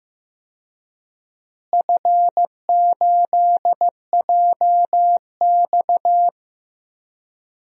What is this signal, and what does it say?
Morse code, heard as a keyed tone, about 15 wpm, 700 Hz. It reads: F8JX
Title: DXCC-100 Call Signs - 4 Characters - Repeated 3x - 15wpm